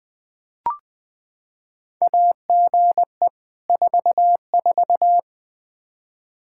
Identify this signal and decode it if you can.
Morse code, heard as a keyed tone, about 20 wpm, 700 Hz. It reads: AGE 44
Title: Practice QSO - Exchange 3 - More Info - 20wpm